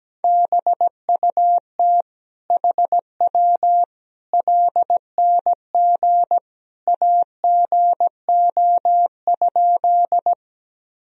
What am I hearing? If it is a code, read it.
Morse code, heard as a keyed tone, about 17 wpm, 700 Hz. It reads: BUT HW LNG AGO?
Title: Sentences from Top 500 Words and Abbreviations 17wpm